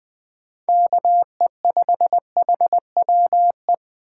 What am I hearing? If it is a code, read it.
Morse code, heard as a keyed tone, about 20 wpm, 700 Hz. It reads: KE5HWE
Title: USA Call Signs - 6 Characters - 3x Character Spacing - 20wpm